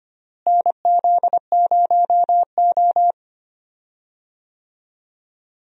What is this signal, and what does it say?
Morse code, heard as a keyed tone, about 25 wpm, 700 Hz. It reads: NZ0O
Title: USA Call Signs - 4 Characters - Repeated 2x - 25wpm